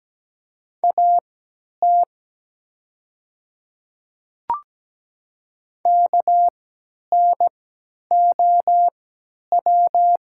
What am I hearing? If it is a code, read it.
Morse code, heard as a keyed tone, about 17 wpm, spaced as for 10 wpm, 700 Hz. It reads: AT KNOW
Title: 3x Character Spacing - Top 300 Words - 17wpm